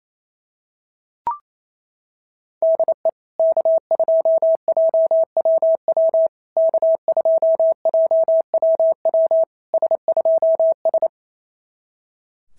Morse code, heard as a keyed tone, about 28 wpm, 650 Hz. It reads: DE K2JWW K2JWW S2S
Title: SOTA QSO Practice 28wpm